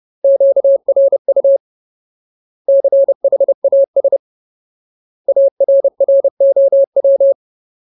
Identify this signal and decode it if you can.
Morse code, heard as a keyed tone, about 30 wpm, 550 Hz. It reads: QRU CHAS ARROW
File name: Ultimate Word List / 30WPM Ultimate